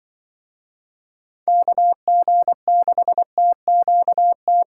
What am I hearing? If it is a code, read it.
Morse code, heard as a keyed tone, about 23 wpm, 700 Hz. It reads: KG6TQT
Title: Speed Racing - USA 2x3 Call Signs 20wpm